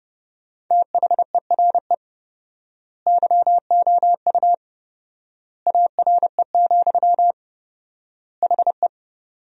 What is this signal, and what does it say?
Morse code, heard as a keyed tone, about 30 wpm, 700 Hz. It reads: THERE YOU ARE, HE
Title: 4x Word-Spacing - Sentences from Top 100 Words - 30wpm